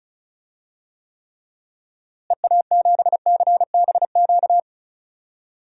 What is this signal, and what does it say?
Morse code, heard as a keyed tone, about 34 wpm, 700 Hz. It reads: EA7CBQ
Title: DXCC-100 Call Signs - 6 Characters - Repeated 3x - 35wpm